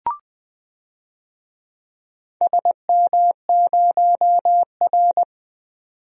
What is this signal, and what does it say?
Morse code, heard as a keyed tone, about 20 wpm, 700 Hz. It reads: SM0R
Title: DXCC-100 Call Signs - 4 Characters - 20wpm